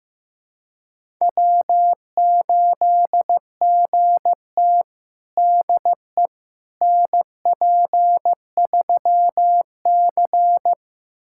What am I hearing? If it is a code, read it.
Morse code, heard as a keyed tone, about 15 wpm, 700 Hz. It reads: W8GT DE NP3C
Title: USA Callsign DE Callsign - 4 Characters - 15wpm